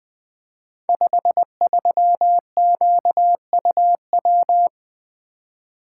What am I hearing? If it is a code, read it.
Morse code, heard as a keyed tone, about 20 wpm, 700 Hz. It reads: 53QUW
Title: Code Groups - 5 Letter:Numbers - 20wpm